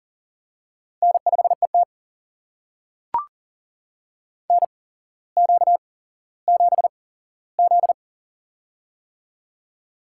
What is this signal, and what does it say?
Morse code, heard as a keyed tone, about 40 wpm, spaced as for 17 wpm, 700 Hz. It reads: N5ET NQ7Z
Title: USA Call Signs - 4 Characters - 8x Character Spacing - 40wpm